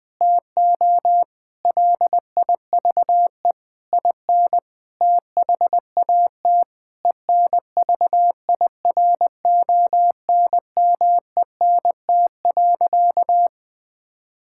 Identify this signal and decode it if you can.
Morse code, heard as a keyed tone, about 21 wpm, 700 Hz. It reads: TO LIVE IN THAT ENVIRONMENT.
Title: Sentences from Top 1000 Words - Review Part 5 - 20wpm